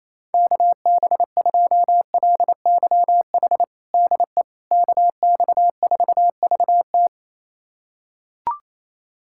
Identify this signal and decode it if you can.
Morse code, heard as a keyed tone, about 28 wpm, 700 Hz. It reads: KB2LYH DE KX4VT
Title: USA Callsign DE Callsign 28wpm